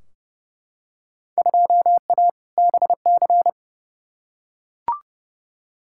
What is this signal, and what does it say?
Morse code, heard as a keyed tone, about 30 wpm, 700 Hz. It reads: 2A BC
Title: Field Day Exchanges - 30wpm